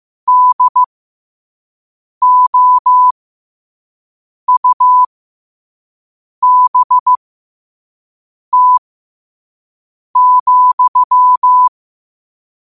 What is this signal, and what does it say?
Morse code, heard as a keyed tone, about 15 wpm, spaced as for 5 wpm, 1000 Hz. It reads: DOUBT,
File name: qotdfiles / 5wpm